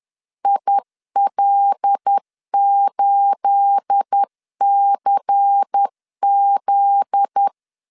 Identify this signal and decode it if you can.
Morse code, heard as a keyed tone, about 10 wpm, 800 Hz. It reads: IL8CZ
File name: LEZIONI / 9 28 10wpm 2842